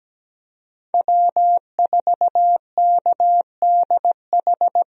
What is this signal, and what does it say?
Morse code, heard as a keyed tone, about 17 wpm, 700 Hz. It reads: W4KDH
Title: USA Call Signs - 5 Characters - Repeated 2x - 17wpm